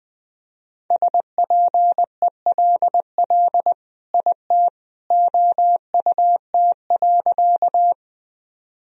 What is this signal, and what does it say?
Morse code, heard as a keyed tone, about 21 wpm, 700 Hz. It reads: SPELL IT OUT.
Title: Sentences from Top 1000 Words - Review Part 1 - 20wpm